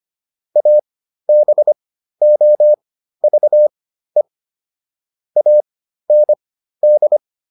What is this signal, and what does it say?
Morse code, heard as a keyed tone, about 25 wpm, spaced as for 13 wpm, 600 Hz. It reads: ABOVE AND